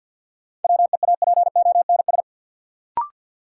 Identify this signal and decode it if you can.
Morse code, heard as a keyed tone, about 53 wpm, 700 Hz. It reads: WEAPONS
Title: Sets of 5 Words Encore - New Word Drill - 50wpm